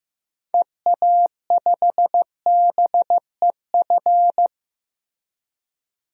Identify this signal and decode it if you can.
Morse code, heard as a keyed tone, about 16 wpm, 700 Hz. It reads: EA5BEF
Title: DXCC-100 Call Signs - 6 Characters - Repeated 2x - 15wpm